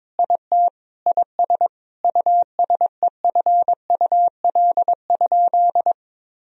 Morse code, heard as a keyed tone, about 22 wpm, 700 Hz. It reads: IT IS USEFUL?
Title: Sentences from Top 1000 Words - Review Part 4 - 22wpm